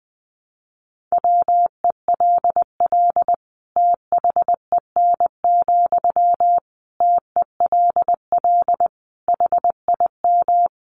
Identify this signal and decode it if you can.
Morse code, heard as a keyed tone, about 20 wpm, 700 Hz. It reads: WELL THEN, TELL HIM
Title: Sentences from Top 200 Words 20wpm